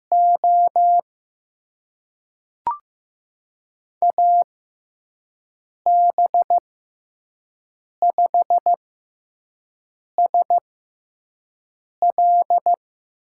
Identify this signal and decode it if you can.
Morse code, heard as a keyed tone, about 15 wpm, spaced as for 5 wpm, 700 Hz. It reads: O AB5SL
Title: USA Call Signs - 5 Characters - 6x Character Spacing - 15wpm